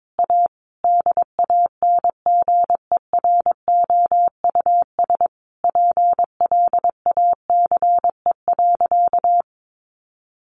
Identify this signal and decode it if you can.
Morse code, heard as a keyed tone, about 22 wpm, 700 Hz. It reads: A DANGEROUS PLACE.
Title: Sentences from Top 1200 Words 22wpm